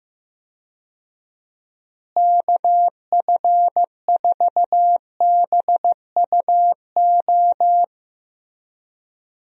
Morse code, heard as a keyed tone, about 15 wpm, 700 Hz. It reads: KF4BUO
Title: USA Call Signs - 6 Characters - Repeated 3x - 15wpm